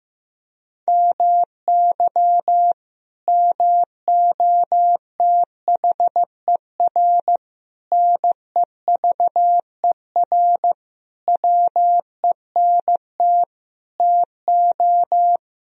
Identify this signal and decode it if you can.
Morse code, heard as a keyed tone, about 15 wpm, 700 Hz. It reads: MY MOTHER NEVER WENT TO
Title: Sentences from Top 1000 Words - Review Part 1 - 15wpm